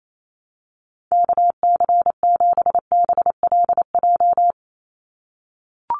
Morse code, heard as a keyed tone, about 28 wpm, 700 Hz. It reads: KC7BLJ
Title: USA Call Signs 28wpm